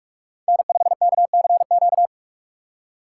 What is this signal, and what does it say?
Morse code, heard as a keyed tone, about 46 wpm, 700 Hz. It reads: N5KCQ
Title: USA Call Signs - 5 Characters - 4x Character Spacing - 45wpm